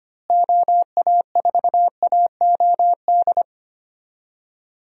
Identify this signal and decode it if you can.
Morse code, heard as a keyed tone, about 25 wpm, 700 Hz. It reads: OA4AOD
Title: USA Call Signs - 6 Characters - Repeated 2x - 25wpm